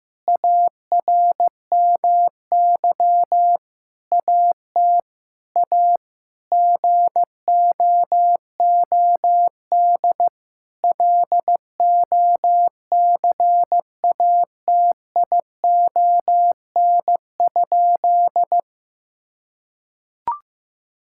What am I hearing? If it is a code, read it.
Morse code, heard as a keyed tone, about 15 wpm, 700 Hz. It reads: ARMY AT A GOOD LOCATION?
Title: Sentences from Top 1000 Words - Review Part 6 - 15wpm